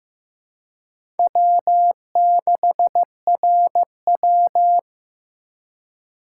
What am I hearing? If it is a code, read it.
Morse code, heard as a keyed tone, about 15 wpm, 700 Hz. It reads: W6RW
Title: USA Call Signs - 4 Characters - 4x Character Spacing - 15wpm